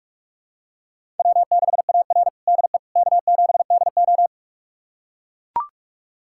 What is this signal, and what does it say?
Morse code, heard as a keyed tone, about 42 wpm, 700 Hz. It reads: W6AR DE K7DO
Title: USA Callsign DE Callsign - 4 Characters - 45wpm